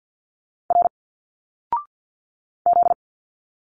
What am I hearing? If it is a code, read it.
Morse code, heard as a keyed tone, about 50 wpm, 700 Hz. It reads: R Z
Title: Single Letters 50wpm